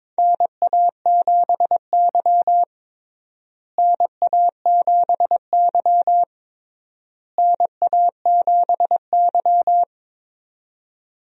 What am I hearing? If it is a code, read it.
Morse code, heard as a keyed tone, about 22 wpm, 700 Hz. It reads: NA7Y NA7Y NA7Y
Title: USA Call Signs - 4 Characters - Repeated 3x - 22wpm